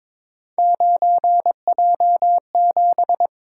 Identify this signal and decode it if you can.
Morse code, heard as a keyed tone, about 22 wpm, 700 Hz. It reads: 9J7
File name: practice / 22wpm / Three Letter-Number Combinations - 22wpm